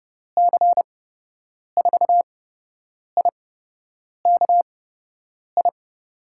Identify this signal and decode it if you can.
Morse code, heard as a keyed tone, about 30 wpm, spaced as for 8 wpm, 700 Hz. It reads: C4IKI
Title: USA Call Signs - 6 Characters - 8x Character Spacing - 30wpm